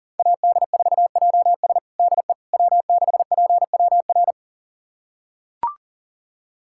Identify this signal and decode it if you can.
Morse code, heard as a keyed tone, about 40 wpm, 690 Hz. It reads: AD4JS DE W6PWR
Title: USA Callsign DE Callsign - 5 Characters - 40wpm